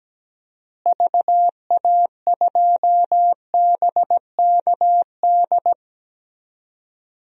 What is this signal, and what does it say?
Morse code, heard as a keyed tone, about 17 wpm, 700 Hz. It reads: VA2BKD
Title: DXCC-100 Call Signs - 6 Characters - Repeated 2x - 17wpm